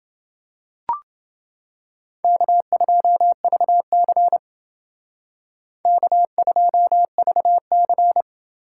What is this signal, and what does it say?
Morse code, heard as a keyed tone, about 28 wpm, 700 Hz. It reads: K2VC K2VC